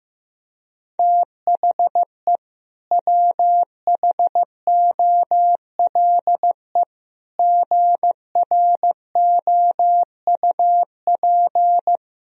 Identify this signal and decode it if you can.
Morse code, heard as a keyed tone, about 16 wpm, 700 Hz. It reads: THE WHOLE GROUP
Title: Sentences from Top 1000 Words - Review Part 1 - 15wpm